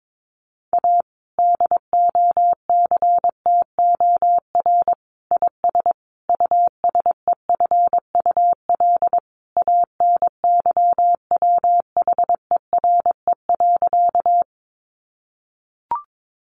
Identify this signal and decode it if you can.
Morse code, heard as a keyed tone, about 22 wpm, 700 Hz. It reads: A DOCTOR IS USEFUL ANYWHERE.
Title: Sentences from Top 1500 Words 22wpm